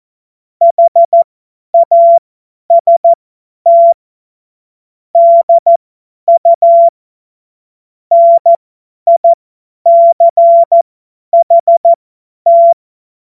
Morse code, heard as a keyed tone, about 14 wpm, spaced as for 10 wpm, 680 Hz. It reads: HAST DU NICHT
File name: Morsekurs / Witze / Witz 28. Dezember 2021, 50 F70